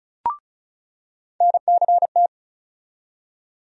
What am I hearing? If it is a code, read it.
Morse code, heard as a keyed tone, about 36 wpm, 700 Hz. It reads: NCT
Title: USA Call Sign Postfixes - 3 Letters 35wpm